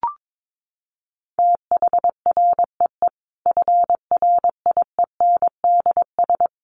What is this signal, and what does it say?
Morse code, heard as a keyed tone, about 22 wpm, 700 Hz. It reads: THREE FRIENDS
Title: Easy Phrases 22wpm